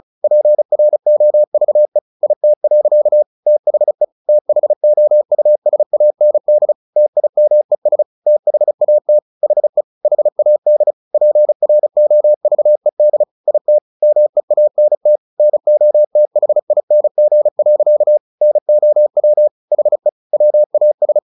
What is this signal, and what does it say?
Morse code, heard as a keyed tone, about 34 wpm, 600 Hz. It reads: PROVE IT. THE THOUSAND TIMES THAT HE HAD PROVED IT MEANT NOTHING. NOW HE WAS
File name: the old man and the sea / the old man and the sea 1 35wpm